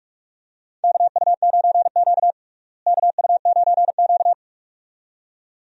Morse code, heard as a keyed tone, about 42 wpm, 700 Hz. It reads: KU9Q KU9Q
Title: USA Call Signs - 4 Characters - Repeated 2x - 45wpm